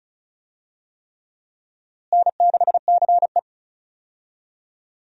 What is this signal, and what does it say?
Morse code, heard as a keyed tone, about 36 wpm, 700 Hz. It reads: N6CE
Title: USA Call Signs - 4 Characters - Repeated 3x - 35wpm